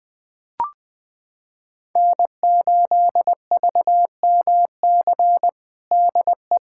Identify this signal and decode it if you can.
Morse code, heard as a keyed tone, about 20 wpm, 700 Hz. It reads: N8VMC DE
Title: USA Callsign DE Callsign - 5 Characters - 20wpm